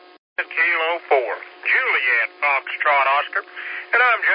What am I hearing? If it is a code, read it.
Emergency Alert System (SAME) tones, heard as harsh buzzing bursts, with speech. It is heard from 0.4 to 3.7 s.